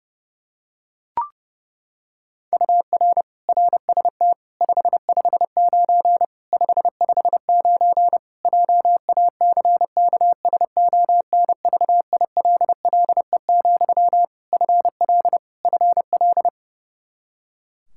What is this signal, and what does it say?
Morse code, heard as a keyed tone, about 30 wpm, 700 Hz. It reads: UR RST 559 559 JACKSONVILLE, FL FL
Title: Practice QSO - Exchange 1 - Beginning - 30wpm